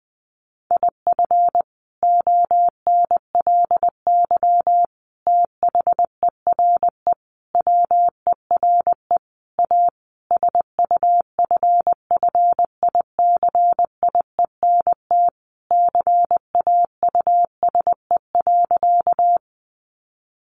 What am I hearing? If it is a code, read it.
Morse code, heard as a keyed tone, about 20 wpm, 700 Hz. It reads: IF ONLY THERE WERE A SUFFICIENT CAUSE.
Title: Sentences from Top 1400 Words 20wpm